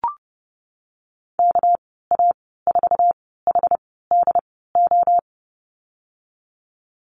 Morse code, heard as a keyed tone, about 30 wpm, spaced as for 17 wpm, 700 Hz. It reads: KA4HDO
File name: Lesson 37g - Letter x - Call Signs 30wpm